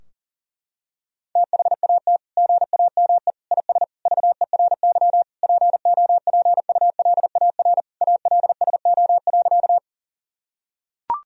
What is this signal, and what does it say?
Morse code, heard as a keyed tone, about 40 wpm, 700 Hz. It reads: THAT GAME IS VERY POPULAR ALSO.
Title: Sentences from Top 1000 Words - Review Part 3 - 40wpm